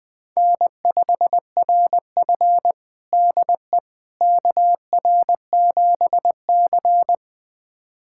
Morse code, heard as a keyed tone, about 20 wpm, 700 Hz. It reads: N5RF DE KR7C
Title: USA Callsign DE Callsign - 4 Characters - 20wpm